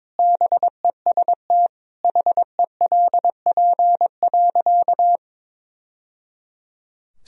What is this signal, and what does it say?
Morse code, heard as a keyed tone, about 21 wpm, 700 Hz. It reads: BEST HELP.
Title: Sentences from Top 1000 Words - Review Part 1 - 22wpm